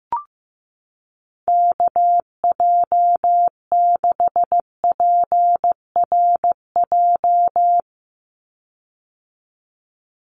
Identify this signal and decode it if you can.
Morse code, heard as a keyed tone, about 15 wpm, 700 Hz. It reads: KJ6PRJ
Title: USA Call Signs 15wpm